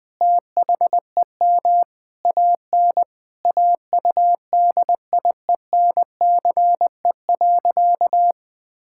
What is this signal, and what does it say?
Morse code, heard as a keyed tone, about 19 wpm, 700 Hz. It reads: THEM AN AUDIENCE.
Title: Sentences from Top 1000 Words - Review Part 5 - 20wpm